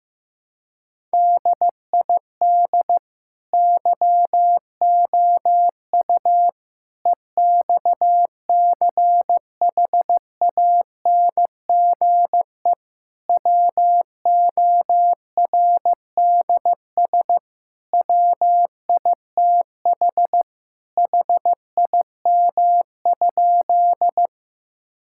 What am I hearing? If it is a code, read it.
Morse code, heard as a keyed tone, about 15 wpm, 700 Hz. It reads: DID YOU EXCHANGE WORDS WITH HIM?
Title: Sentences from Top 1000 Words - Review Part 6 - 15wpm